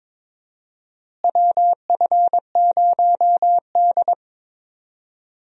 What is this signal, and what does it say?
Morse code, heard as a keyed tone, about 22 wpm, 700 Hz. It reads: WF0D
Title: USA Call Signs - 4 Characters - Repeated 3x - 22wpm